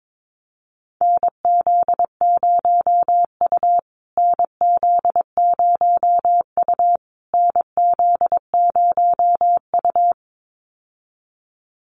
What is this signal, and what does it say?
Morse code, heard as a keyed tone, about 22 wpm, 700 Hz. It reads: NZ0U NZ0U NZ0U
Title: USA Call Signs Repeated 3x 22wpm